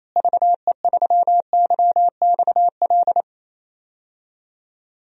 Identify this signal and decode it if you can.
Morse code, heard as a keyed tone, about 28 wpm, 700 Hz. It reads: VE3YXL
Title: DXCC-100 Call Signs - 6 Characters - Repeated 2x - 28wpm